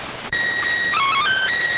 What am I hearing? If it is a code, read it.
SSTV (slow-scan television), PD 290.